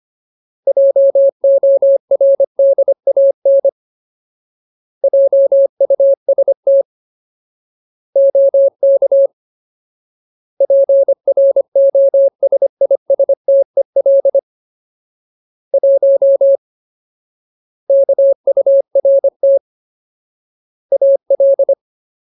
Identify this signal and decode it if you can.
Morse code, heard as a keyed tone, about 25 wpm, 550 Hz. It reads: JORDAN JUST OK PROSISTEL 1 KURT AL
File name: Ultimate Word List / 25WPM Ultimate